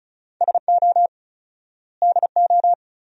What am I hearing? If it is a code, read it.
Morse code, heard as a keyed tone, about 34 wpm, 700 Hz. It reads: SO DO
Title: 4x Word-Spacing - Sentences from Top 400 Words - 35wpm